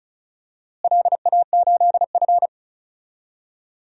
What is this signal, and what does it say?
Morse code, heard as a keyed tone, about 34 wpm, 700 Hz. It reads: LA8F
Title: DXCC-100 Call Signs - 4 Characters - Repeated 3x - 35wpm